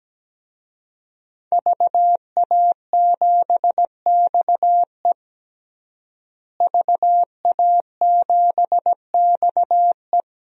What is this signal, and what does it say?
Morse code, heard as a keyed tone, about 17 wpm, 700 Hz. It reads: VA7XE VA7XE
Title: DXCC-100 Call Signs - 5 Characters - Repeated 3x - 17wpm